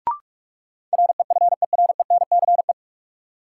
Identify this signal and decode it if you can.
Morse code, heard as a keyed tone, about 46 wpm, 700 Hz. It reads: REFERENCE
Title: Sentences from Top 1200 Words - New Word Drill Alt - 45wpm